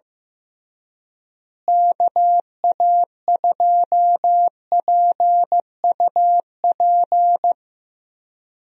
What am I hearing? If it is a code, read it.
Morse code, heard as a keyed tone, about 16 wpm, 700 Hz. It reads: KA2PUP
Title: USA Call Signs - 6 Characters - Repeated 3x - 15wpm